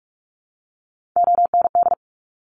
Morse code, heard as a keyed tone, about 46 wpm, 700 Hz. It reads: OND
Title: Top Three Letter Combinations in Words 45wpm